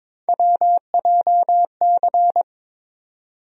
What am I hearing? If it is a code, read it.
Morse code, heard as a keyed tone, about 22 wpm, 700 Hz. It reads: WJC
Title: USA Call Sign Postfixes - 3 Letters 22wpm